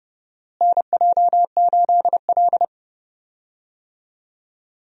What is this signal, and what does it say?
Morse code, heard as a keyed tone, about 30 wpm, 700 Hz. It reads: NJ8L
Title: USA Call Signs - 4 Characters - Repeated 3x - 30wpm